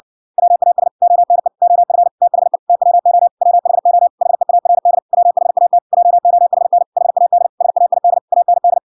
Morse code, heard as a keyed tone, about 58 wpm, 700 Hz. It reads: WAS ONE OF THE TWO WHO HEARD WHAT JOHN HAD SAID AND